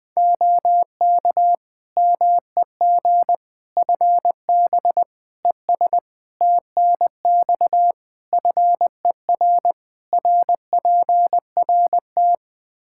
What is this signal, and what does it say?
Morse code, heard as a keyed tone, about 20 wpm, 700 Hz. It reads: OK MEG FB ES TNX FER RPRT
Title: Practice QSO - Exchange 2 - Info - 20wpm